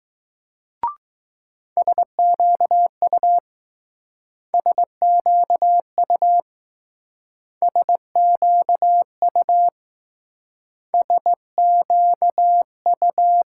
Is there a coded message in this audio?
Yes: Morse code from 1.8 to 13.4 s, heard as a keyed tone, about 20 wpm, 700 Hz. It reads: SQU SQU SQU SQU